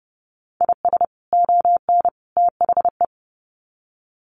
Morse code, heard as a keyed tone, about 30 wpm, 700 Hz. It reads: IS ON THE
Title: Sets of 3 Words 30wpm